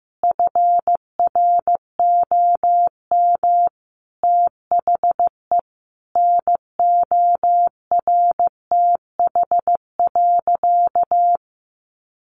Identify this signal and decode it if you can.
Morse code, heard as a keyed tone, about 15 wpm, 700 Hz. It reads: FROM THE NORTH.
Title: Sentences from Top 1200 Words 15wpm